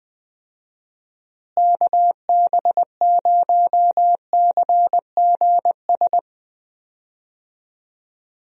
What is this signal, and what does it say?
Morse code, heard as a keyed tone, about 20 wpm, 700 Hz. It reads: KB0CGS
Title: USA Call Signs - 6 Characters - Repeated 2x - 20wpm